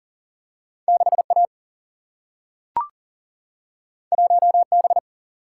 Morse code, heard as a keyed tone, about 39 wpm, 700 Hz. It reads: 6A 1B
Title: Single-Letter Single-Number 40wpm